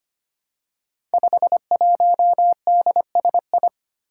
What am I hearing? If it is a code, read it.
Morse code, heard as a keyed tone, about 25 wpm, 700 Hz. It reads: 51DSI
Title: Five Challenging Letter-Number Combinations 25wpm